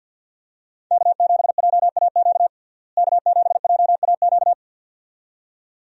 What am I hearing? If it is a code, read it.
Morse code, heard as a keyed tone, about 52 wpm, 700 Hz. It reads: K7JAQ K7JAQ
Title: USA Call Signs - 5 Characters - Repeated 2x - 50wpm